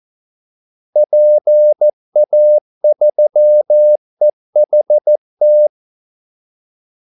Morse code, heard as a keyed tone, about 14 wpm, 600 Hz. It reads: PA3EHT
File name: Callsigns-14wpm